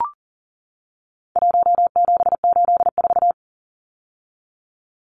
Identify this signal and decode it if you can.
Morse code, heard as a keyed tone, about 40 wpm, 700 Hz. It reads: 1784
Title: Numbers - 4-Digit 40wpm